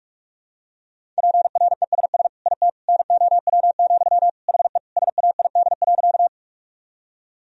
Morse code, heard as a keyed tone, about 46 wpm, 700 Hz. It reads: PRESS IT NOW, HE SAID.
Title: Sentences from Top 1000 Words - Review Part 1 - 45wpm